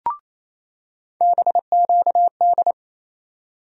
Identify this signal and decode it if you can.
Morse code, heard as a keyed tone, about 28 wpm, 700 Hz. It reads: BQD
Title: USA Call Sign Postfixes - 3 Letters 28wpm